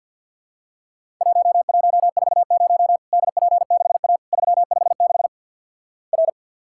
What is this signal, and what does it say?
Morse code, heard as a keyed tone, about 50 wpm, 700 Hz. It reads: 1140 DP6A 356 R
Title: Worked All Europe DX Contest - QTC Exchange with Repeats - 50wpm